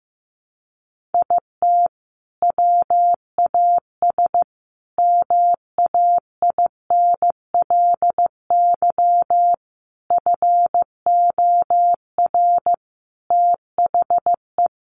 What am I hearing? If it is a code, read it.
Morse code, heard as a keyed tone, about 15 wpm, 700 Hz. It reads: IT WAS MAINLY FOR THE
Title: Sentences from Top 1400 Words 15wpm